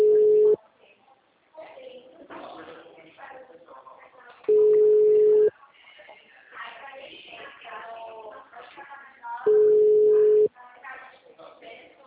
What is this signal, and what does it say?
Ringback tone (Europe, 425 Hz), with speech.